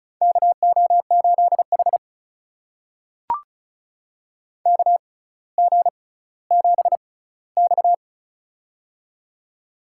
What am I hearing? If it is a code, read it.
Morse code, heard as a keyed tone, about 34 wpm, spaced as for 18 wpm, 700 Hz. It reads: KO8H KG7X
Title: USA Call Signs - 4 Characters - 6x Character Spacing - 35wpm